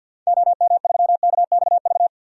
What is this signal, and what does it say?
Morse code, heard as a keyed tone, about 46 wpm, 700 Hz. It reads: OM3KXV
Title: DXCC-100 Call Signs - 6 Characters - 50wpm